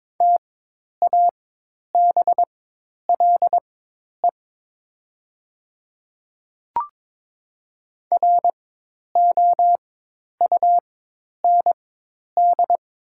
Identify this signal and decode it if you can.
Morse code, heard as a keyed tone, about 23 wpm, spaced as for 10 wpm, 700 Hz. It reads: TABLE ROUND
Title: 4x Character Spacing - Top 500 Words - 22wpm